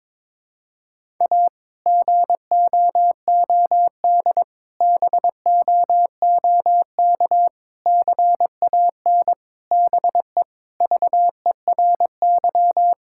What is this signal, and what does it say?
Morse code, heard as a keyed tone, about 22 wpm, 700 Hz. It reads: A GOOD BOOK CAN BE VERY
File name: practice / 22wpm / Sentences with 2-Word Phrases - 22wpm